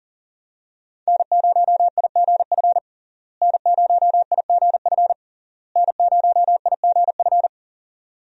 Morse code, heard as a keyed tone, about 40 wpm, 700 Hz. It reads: N0IGF N0IGF N0IGF
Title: USA Call Signs - 5 Characters - Repeated 3x - 40wpm